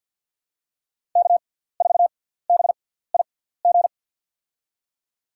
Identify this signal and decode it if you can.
Morse code, heard as a keyed tone, about 53 wpm, spaced as for 17 wpm, 700 Hz. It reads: K4BIG
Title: USA Call Signs - 5 Characters - 6x Character Spacing - 50wpm